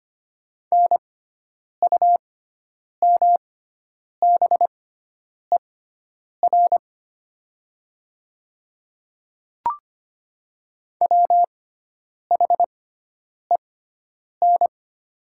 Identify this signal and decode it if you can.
Morse code, heard as a keyed tone, about 25 wpm, spaced as for 9 wpm, 700 Hz. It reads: NUMBER WHEN
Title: 6x Character Spacing - Top 100 Words - 25wpm